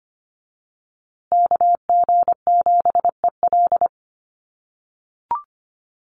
Morse code, heard as a keyed tone, about 25 wpm, 700 Hz. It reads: KG7EL
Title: USA Call Signs Repeated 3x 25wpm